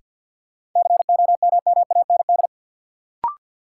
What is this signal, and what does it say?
Morse code, heard as a keyed tone, about 52 wpm, 700 Hz. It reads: COMMAND
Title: Sentences from Top 1600 Words - New Word Drill - 50wpm